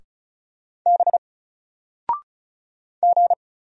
Morse code, heard as a keyed tone, about 36 wpm, 700 Hz. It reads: B G
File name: Single-Letter-Number-35wpm